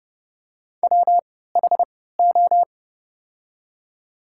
Morse code, heard as a keyed tone, about 30 wpm, spaced as for 17 wpm, 700 Hz. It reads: WHO
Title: 3x Character Spacing - Top 500 Words - 30wpm